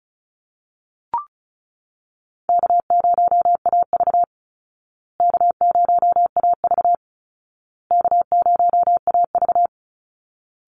Morse code, heard as a keyed tone, about 34 wpm, 700 Hz. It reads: K0AV K0AV K0AV
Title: Call Signs Repeated 2x - 35wpm